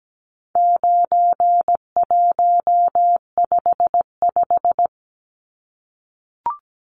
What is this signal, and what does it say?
Morse code, heard as a keyed tone, about 17 wpm, 700 Hz. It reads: 9155
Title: Numbers - 4-Digit 17wpm